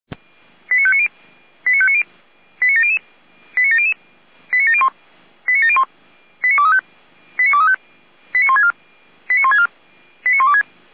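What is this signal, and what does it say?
Five-tone selective call (ZVEI), heard as quick runs of five tones, tones of about 70 ms. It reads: 89590 89590 89700 89700 89701 89701 89235 89235 89164 89164 89126
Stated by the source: ZVEI five-tone call 08911 received by SDR.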